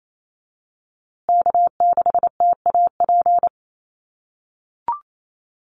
Morse code, heard as a keyed tone, about 28 wpm, 700 Hz. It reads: K6TAP
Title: USA Call Signs Repeated 2x 28wpm